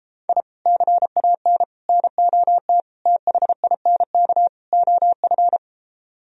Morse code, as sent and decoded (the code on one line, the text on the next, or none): .. / -.-. .- -. / -. --- - / - .... .. -. -.- / --- ..-.
I CAN NOT THINK OF